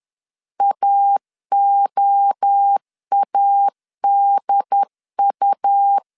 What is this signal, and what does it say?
Morse code, heard as a keyed tone, about 10 wpm, 800 Hz. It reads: AOADU
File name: CW / DET / 1 3 1500C S C 10wpm 3335